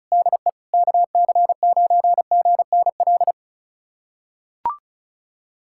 Morse code, heard as a keyed tone, about 36 wpm, 700 Hz. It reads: DE KC9GNL
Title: USA Callsign DE Callsign 35wpm